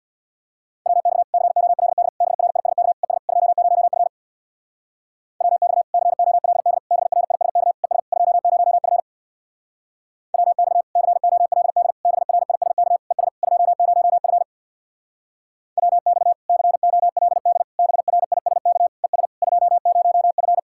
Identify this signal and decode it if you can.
Morse code, heard as a keyed tone, about 64 wpm, 700 Hz. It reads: WX COLD BRISK ES 20F WX COLD BRISK ES 20F WX COLD BRISK ES 20F WX COLD BRISK ES 20F